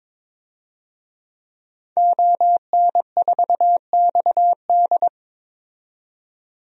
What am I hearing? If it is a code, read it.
Morse code, heard as a keyed tone, about 22 wpm, 700 Hz. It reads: ON4XD
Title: DXCC-100 Call Signs - 5 Characters - 22wpm